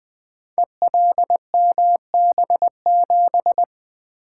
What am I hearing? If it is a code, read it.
Morse code, heard as a keyed tone, about 20 wpm, 700 Hz. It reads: ELMB7
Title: Code Groups - 5 Letter:Numbers - 20wpm